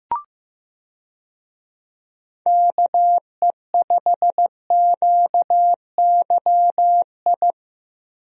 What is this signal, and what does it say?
Morse code, heard as a keyed tone, about 15 wpm, 700 Hz. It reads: KE5QYI
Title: USA Call Signs - 6 Characters - Repeated 2x - 15wpm